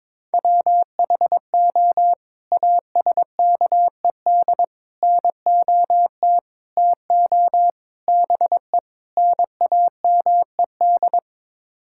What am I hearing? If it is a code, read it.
Morse code, heard as a keyed tone, about 21 wpm, 700 Hz. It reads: WHO ASKED NOT TO BE NAMED
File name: practice / 22wpm / Sets of 6 Words - 22wpm